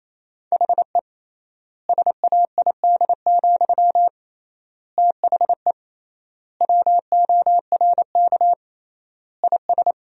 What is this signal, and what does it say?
Morse code, heard as a keyed tone, about 28 wpm, 700 Hz. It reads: HE SAID, THE WORK IS
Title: 3x Word-Spacing - Sentences from Top 400 Words - 28wpm